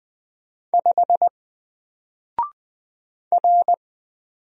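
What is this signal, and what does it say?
Morse code, heard as a keyed tone, about 20 wpm, 700 Hz. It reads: 5 R
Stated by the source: Single letter and number Morse drill at 20 wpm, from N4LCD.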